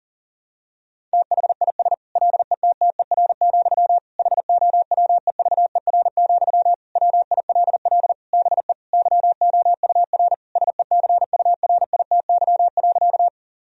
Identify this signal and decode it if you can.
Morse code, heard as a keyed tone, about 40 wpm, 700 Hz. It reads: THIS LETTER, HOWEVER, WILL BE YOUR SECURITY.
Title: Sentences from Top 1000 Words - Review Part 4 - 40wpm